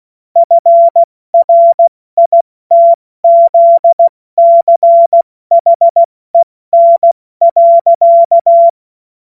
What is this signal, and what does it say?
Morse code, heard as a keyed tone, about 16 wpm, 680 Hz. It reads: FRITZCHEN.
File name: Morsekurs / Witze / Witz 21. September 2021, 70 F80